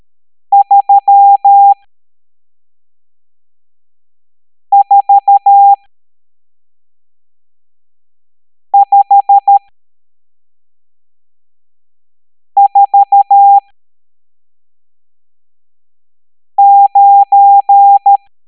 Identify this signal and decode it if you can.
Morse code, heard as a keyed tone, about 13 wpm, spaced as for 3 wpm, 800 Hz. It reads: 34549
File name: hamdata / L12-EX02-CHIFFRES